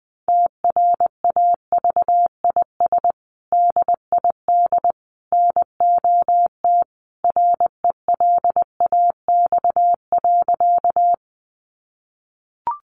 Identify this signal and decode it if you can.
Morse code, heard as a keyed tone, about 20 wpm, 700 Hz. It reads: TRAVIS DID NOT RELAX.
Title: A Sound Of Thunder 20wpm